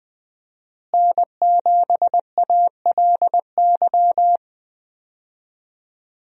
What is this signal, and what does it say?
Morse code, heard as a keyed tone, about 20 wpm, 700 Hz. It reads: N7ALY
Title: USA Call Signs - 5 Characters - Repeated 2x - 20wpm